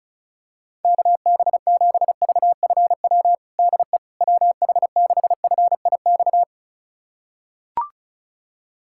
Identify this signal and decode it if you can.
Morse code, heard as a keyed tone, about 36 wpm, 700 Hz. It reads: KB7VFW DE WH6FIX
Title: USA Callsign DE Callsign - 6 Characters - 35wpm